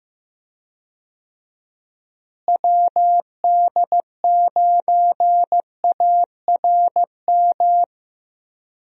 Morse code, heard as a keyed tone, about 15 wpm, 700 Hz. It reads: WD9ARM
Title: USA Call Signs - 6 Characters - Repeated 2x - 15wpm